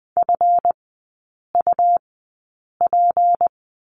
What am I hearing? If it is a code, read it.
Morse code, heard as a keyed tone, about 20 wpm, spaced as for 8 wpm, 700 Hz. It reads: FUP
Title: 166 Lesson 27e - Number 7 - Three Characters 20wpm